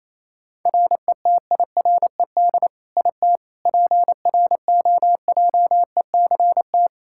Morse code, heard as a keyed tone, about 27 wpm, 700 Hz. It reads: RETIRED IT PROJECT
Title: Practice QSO - Exchange 3 - More Info - 28wpm